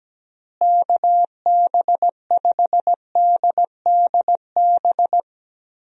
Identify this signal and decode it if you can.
Morse code, heard as a keyed tone, about 17 wpm, 700 Hz. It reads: KB5DDB
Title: USA Call Signs - 6 Characters - Repeated 2x - 17wpm